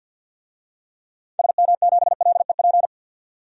Morse code, heard as a keyed tone, about 52 wpm, 700 Hz. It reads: SM7LEP
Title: DXCC-100 Call Signs - 6 Characters - 50wpm